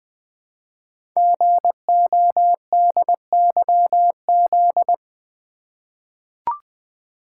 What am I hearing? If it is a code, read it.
Morse code, heard as a keyed tone, about 20 wpm, 700 Hz. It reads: GODYZ
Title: Code Groups - 5 Letters - 20wpm